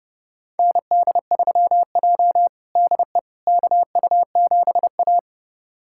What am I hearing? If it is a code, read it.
Morse code, heard as a keyed tone, about 30 wpm, 700 Hz. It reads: ND3J DE KU7A
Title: USA Callsign DE Callsign - 4 Characters - 30wpm